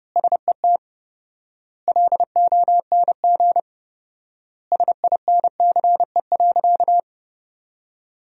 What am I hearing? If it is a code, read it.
Morse code, heard as a keyed tone, about 30 wpm, 700 Hz. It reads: SET LONG SINCE.
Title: 4x Word-Spacing - Sentences from Top 300 Words - 30wpm